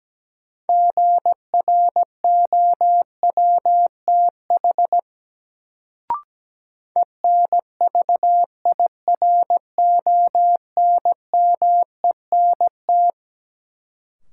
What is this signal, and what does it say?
Morse code, heard as a keyed tone, about 17 wpm, 700 Hz. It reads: GROWTH ENVIRONMENT
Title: Sentences from Top 1000 Words - Review Part 5 - New Word Drill - 17wpm